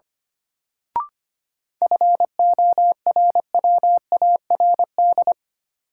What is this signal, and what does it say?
Morse code, heard as a keyed tone, about 25 wpm, 700 Hz. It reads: FORWARD
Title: Sentences from Top 1300 Words - New Word Drill Alt - 25wpm